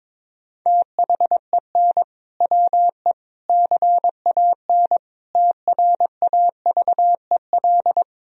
Morse code, heard as a keyed tone, about 22 wpm, 700 Hz. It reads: THEN WE CAN TRAVEL
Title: Sentences from Top 1000 Words - Review Part 2 - 22wpm